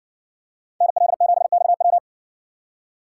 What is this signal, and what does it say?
Morse code, heard as a keyed tone, about 56 wpm, 700 Hz. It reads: NL7XW